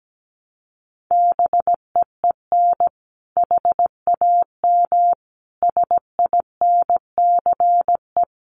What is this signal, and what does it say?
Morse code, heard as a keyed tone, about 17 wpm, 700 Hz. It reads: BEEN HAM SINCE
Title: Common QSO Phrases 17wpm